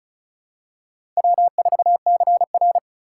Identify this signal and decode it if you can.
Morse code, heard as a keyed tone, about 34 wpm, 700 Hz. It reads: W4CR
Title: USA Call Signs - 6x Character Spacing - 35wpm